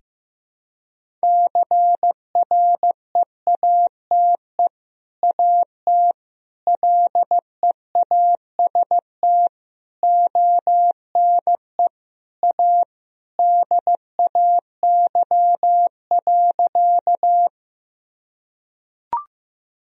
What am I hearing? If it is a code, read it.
Morse code, heard as a keyed tone, about 15 wpm, 700 Hz. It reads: CREATE AT LEAST ONE A DAY.
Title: Sentences from Top 1000 Words - Review Part 3 - 15wpm